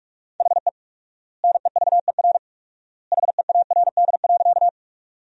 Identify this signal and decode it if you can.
Morse code, heard as a keyed tone, about 42 wpm, 700 Hz. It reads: HE NEVER HEARD.
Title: 4x Word-Spacing - Sentences from Top 400 Words - 45wpm